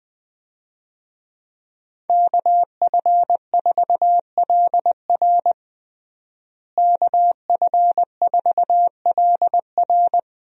Morse code, heard as a keyed tone, about 21 wpm, 700 Hz. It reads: KF4LR KF4LR
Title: USA Call Signs - 5 Characters - Repeated 3x - 20wpm